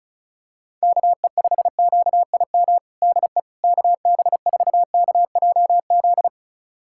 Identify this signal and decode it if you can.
Morse code, heard as a keyed tone, about 36 wpm, 700 Hz. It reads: KE5QIM DE KB4KJZ
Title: USA Callsign DE Callsign - 6 Characters - 35wpm